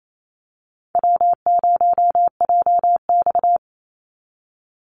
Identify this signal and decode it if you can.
Morse code, heard as a keyed tone, about 28 wpm, 700 Hz. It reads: W0JX
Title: Easy Call Signs Repeated 3x - 28wpm